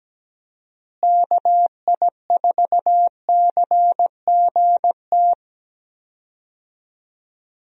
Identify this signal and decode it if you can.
Morse code, heard as a keyed tone, about 17 wpm, 700 Hz. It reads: KI4CGT
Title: USA Call Signs - 6 Characters - Repeated 2x - 17wpm